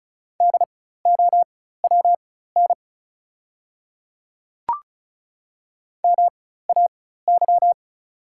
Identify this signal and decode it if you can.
Morse code, heard as a keyed tone, about 36 wpm, spaced as for 16 wpm, 690 Hz. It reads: DOWN MAY
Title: 4x Character Spacing - Top 200 Words - 35wpm